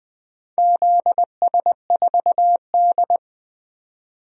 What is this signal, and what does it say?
Morse code, heard as a keyed tone, about 20 wpm, 700 Hz. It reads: ZS4D
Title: DXCC-100 Call Signs - 4 Characters - 20wpm